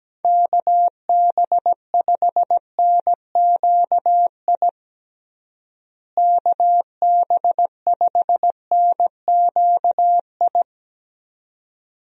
Morse code, heard as a keyed tone, about 17 wpm, 700 Hz. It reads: KB5NQI KB5NQI
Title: USA Call Signs - 6 Characters - Repeated 2x - 17wpm